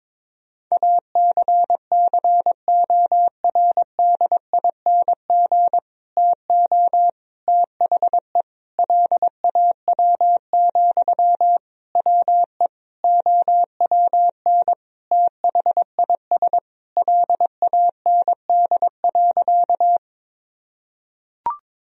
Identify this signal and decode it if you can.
Morse code, heard as a keyed tone, about 21 wpm, 700 Hz. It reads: ACCORDING TO THE LAW, WE OWN THIS LAND.
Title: Sentences from Top 1000 Words - Review Part 4 - 22wpm